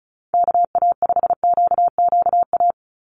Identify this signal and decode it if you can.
Morse code, heard as a keyed tone, about 34 wpm, 700 Hz. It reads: KA5QQA
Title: NA SOTA Chaser Callsigns 35wpm